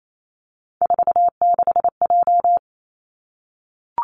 Morse code, heard as a keyed tone, about 28 wpm, 690 Hz. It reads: 46J
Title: Three Challenging Letter-Number Combinations 28wpm